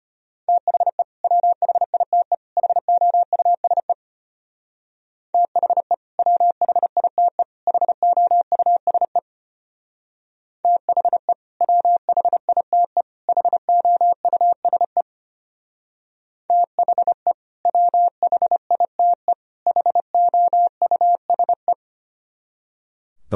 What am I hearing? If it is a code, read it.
Morse code, heard as a keyed tone, about 32 wpm, 700 Hz. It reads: THE WHITE HOUSE THE WHITE HOUSE THE WHITE HOUSE THE WHITE HOUSE